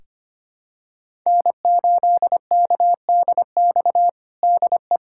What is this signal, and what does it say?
Morse code, heard as a keyed tone, about 25 wpm, 700 Hz. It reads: N8KDX DE
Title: USA Callsign DE Callsign - 5 Characters - 25wpm